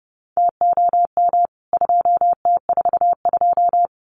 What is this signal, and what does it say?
Morse code, heard as a keyed tone, about 30 wpm, 700 Hz. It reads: TOM 2T42
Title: CWT Exchange 30wpm